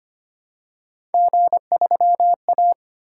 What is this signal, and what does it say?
Morse code, heard as a keyed tone, about 25 wpm, 700 Hz. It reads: G3A
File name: Three Letter-Number Combinations - 25wpm